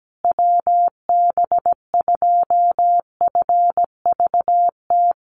Morse code, heard as a keyed tone, about 17 wpm, 700 Hz. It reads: WB2FVT
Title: USA Call Signs Repeated 3x 17wpm